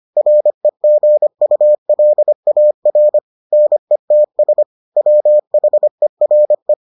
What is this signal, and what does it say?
Morse code, heard as a keyed tone, about 25 wpm, 600 Hz. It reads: REGULAR NETS WHERE